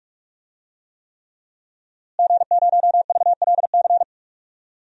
Morse code, heard as a keyed tone, about 44 wpm, 700 Hz. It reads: G0VLC
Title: DXCC-100 Call Signs - 5 Characters - Repeated 3x - 45wpm